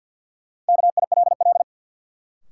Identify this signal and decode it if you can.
Morse code, heard as a keyed tone, about 51 wpm, 700 Hz. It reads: KILL
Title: Sentences from Top 1400 Words - New Word Drill - 50wpm